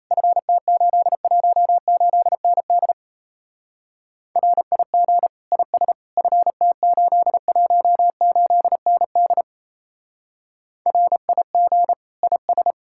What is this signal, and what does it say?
Morse code, heard as a keyed tone, about 35 wpm, 700 Hz. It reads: FT818ND RIG IS FT818ND RIG IS
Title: Speed-Racing Common QSO Phrases - 28wpm